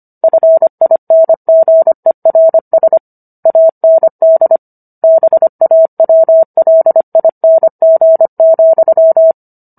Morse code, heard as a keyed tone, about 25 wpm, 650 Hz. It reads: FINGERS AND BAWLING,